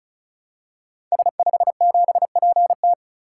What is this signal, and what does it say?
Morse code, heard as a keyed tone, about 36 wpm, 700 Hz. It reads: S57PT
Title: Code Groups - 5 Letter:Numbers - 35wpm